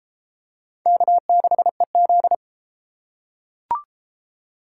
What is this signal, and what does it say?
Morse code, heard as a keyed tone, about 32 wpm, 700 Hz. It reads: K6EZ